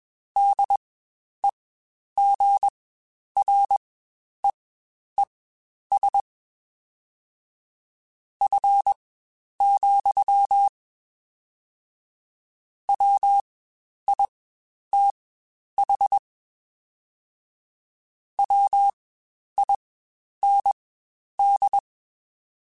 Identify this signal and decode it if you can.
Morse code, heard as a keyed tone, about 21 wpm, spaced as for 10 wpm, 800 Hz. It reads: DEGREES F, WITH WIND